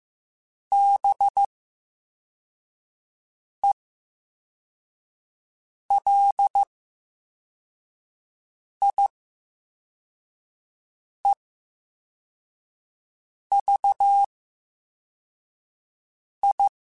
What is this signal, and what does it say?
Morse code, heard as a keyed tone, about 15 wpm, spaced as for 4 wpm, 800 Hz. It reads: BELIEVI
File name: Special INGs / Special INGs-mp3s / Slow ing @13